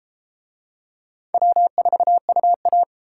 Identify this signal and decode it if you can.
Morse code, heard as a keyed tone, about 28 wpm, 700 Hz. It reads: W4UA
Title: Speed Racing - USA 1x2 Call Signs 22wpm